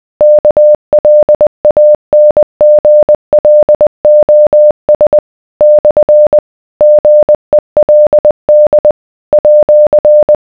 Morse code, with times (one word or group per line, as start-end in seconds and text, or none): 0.2-5.2 KLANGLOS
5.6-6.4 /
6.8-8.9 GELD
9.3-10.4 @